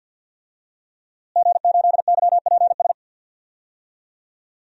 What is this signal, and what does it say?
Morse code, heard as a keyed tone, about 51 wpm, 700 Hz. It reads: G8YPS
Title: DXCC-100 Call Signs - 5 Characters - Repeated 2x - 50wpm